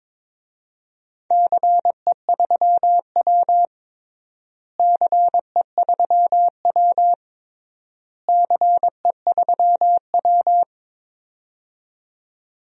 Morse code, heard as a keyed tone, about 22 wpm, 700 Hz. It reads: CE3W CE3W CE3W
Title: DXCC-100 Call Signs - 4 Characters - Repeated 3x - 22wpm